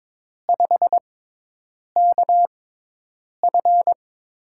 Morse code, heard as a keyed tone, about 22 wpm, spaced as for 8 wpm, 700 Hz. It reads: 5KF
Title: USA Call Signs - 4 Characters - 6x Character Spacing - 22wpm